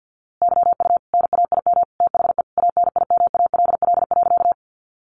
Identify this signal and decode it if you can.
Morse code, heard as a keyed tone, about 50 wpm, 700 Hz. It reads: YU NAIM THE ANIMALL.
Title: A Sound Of Thunder 50wpm